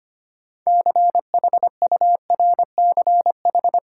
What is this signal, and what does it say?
Morse code, heard as a keyed tone, about 25 wpm, 700 Hz. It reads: CHURCH
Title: Sets of 5 Words - New Word Drill - 25wpm